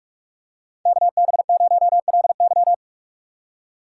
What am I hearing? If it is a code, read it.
Morse code, heard as a keyed tone, about 46 wpm, 700 Hz. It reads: KB0LY